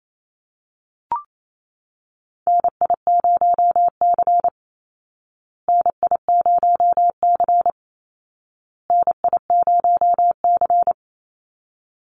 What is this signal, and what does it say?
Morse code, heard as a keyed tone, about 28 wpm, 700 Hz. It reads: NI0C NI0C NI0C
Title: Call Signs Repeated 2x - 28wpm